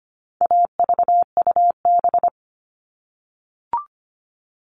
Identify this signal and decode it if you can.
Morse code, heard as a keyed tone, about 25 wpm, 700 Hz. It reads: AVUB
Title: All Cut Numbers - 4-Digit 25wpm